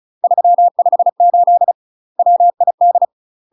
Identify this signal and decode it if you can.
Morse code, heard as a keyed tone, about 36 wpm, 700 Hz. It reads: 358 WID
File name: UK MASTER QSOs TRAINING / 35WPM / UK MASTER QSOs 0033